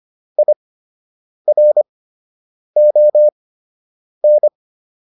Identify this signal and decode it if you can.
Morse code, heard as a keyed tone, about 25 wpm, spaced as for 8 wpm, 600 Hz. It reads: IRON